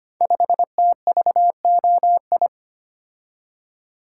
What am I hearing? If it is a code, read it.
Morse code, heard as a keyed tone, about 25 wpm, 700 Hz. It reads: 5TVOI
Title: Code Groups - 5 Letter:Numbers - 25wpm